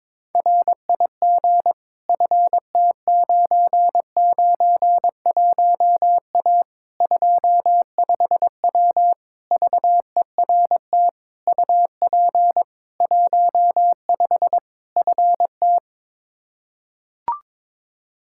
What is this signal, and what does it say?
Morse code, heard as a keyed tone, about 22 wpm, 700 Hz. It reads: RIG FT991A 25W VERT UP 15 FT
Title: Practice QSO - Exchange 2 - Info - 22wpm